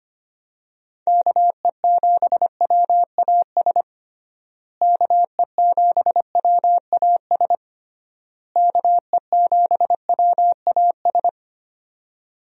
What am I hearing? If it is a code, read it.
Morse code, heard as a keyed tone, about 25 wpm, 700 Hz. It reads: KE7WAS KE7WAS KE7WAS
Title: USA Call Signs - 6 Characters - Repeated 3x - 25wpm